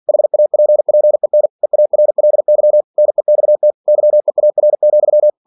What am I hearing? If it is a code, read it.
Morse code, heard as a keyed tone, about 46 wpm, 600 Hz. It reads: HAPPEN EARLY NEXT YEAR,
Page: Never 47wpm